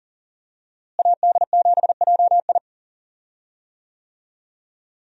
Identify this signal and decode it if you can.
Morse code, heard as a keyed tone, about 40 wpm, 700 Hz. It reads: AD7JI
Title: USA Call Signs - 5 Characters - Repeated 2x - 40wpm